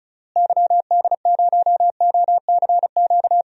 Morse code, heard as a keyed tone, about 36 wpm, 700 Hz. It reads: YD0OCQ
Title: DXCC-100 Call Signs - 6 Characters - Repeated 3x - 35wpm